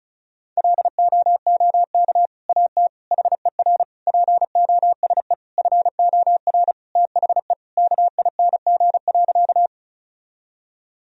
Morse code, heard as a keyed tone, about 34 wpm, 700 Hz. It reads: LOOK AT HER POSE FOR THE KING.
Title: Sentences from Top 1000 Words - Review Part 2 - 35wpm